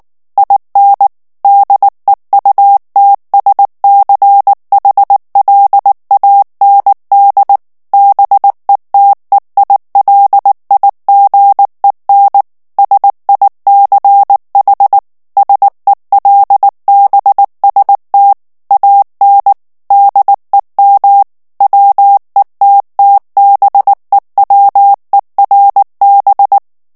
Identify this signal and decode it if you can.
Morse code, heard as a keyed tone, about 19 wpm, 800 Hz. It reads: IN DEUTSCHLAND BETEILIGEN SICH SELBST AN DEM WETTBEWERB
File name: morse mp3 / ALLEMAND texte clair 2 vitesse 18 mots minute